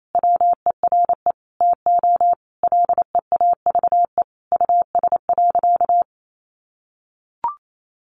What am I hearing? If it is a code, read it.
Morse code, heard as a keyed tone, about 28 wpm, 700 Hz. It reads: WERE TO LEAVE US.
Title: Sentences from Top 400 Words 28wpm